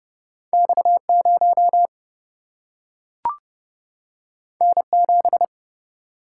Morse code, heard as a keyed tone, about 30 wpm, 700 Hz. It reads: X 0 N 7
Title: Single-Letter Single-Number 30wpm